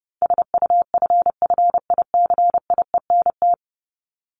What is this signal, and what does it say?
Morse code, heard as a keyed tone, about 30 wpm, 700 Hz. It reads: SUFFICIENT
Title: LY Suffix - Top 100 Words with Base Words 30wpm